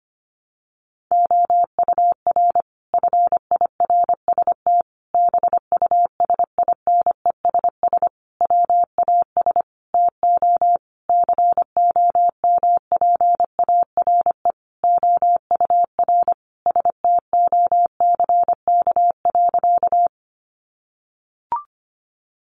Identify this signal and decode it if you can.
Morse code, heard as a keyed tone, about 25 wpm, 700 Hz. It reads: OUR FIRST BUSINESS WAS TO COMPARE OUR STOCK.
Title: Sentences from Top 1300 Words 25wpm